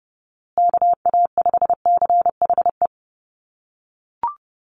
Morse code, heard as a keyed tone, about 30 wpm, 700 Hz. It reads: KA5CHE
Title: USA Call Signs Repeated 3x 30wpm